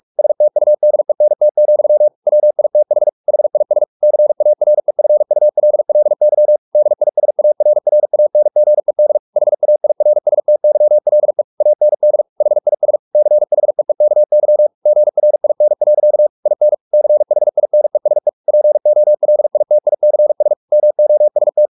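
Morse code, heard as a keyed tone, about 44 wpm, 600 Hz. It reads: STUDENT, WITH HIS CAREFULLY DISARRANGED HAIRSTYLE AND HIS CHEEKY GRIN. IN CHINESE POLITICS MOST
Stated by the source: Morse practice at 45 wpm, the novel Never by Ken Follett in English, from CW med Gnister.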